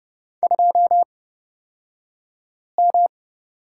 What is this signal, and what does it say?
Morse code, heard as a keyed tone, about 30 wpm, spaced as for 5 wpm, 690 Hz. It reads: 2M
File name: Lesson 36d - BT - Two Characters 30wpm